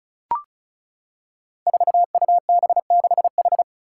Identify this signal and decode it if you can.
Morse code, heard as a keyed tone, about 36 wpm, 700 Hz. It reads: 4UB6H
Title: Five Challenging Letter-Number Combinations 35wpm